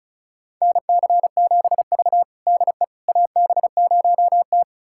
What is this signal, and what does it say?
Morse code, heard as a keyed tone, about 36 wpm, 700 Hz. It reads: NC7V DE AB0T
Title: USA Callsign DE Callsign - 4 Characters - 35wpm